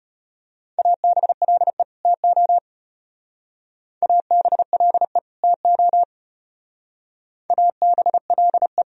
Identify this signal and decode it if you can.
Morse code, heard as a keyed tone, about 32 wpm, 700 Hz. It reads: ABLE TO ABLE TO ABLE